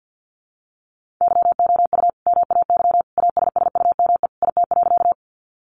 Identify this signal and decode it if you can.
Morse code, heard as a keyed tone, about 50 wpm, 700 Hz. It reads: YOU MAY ASSUME IT.
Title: Sentences from Top 1500 Words 50wpm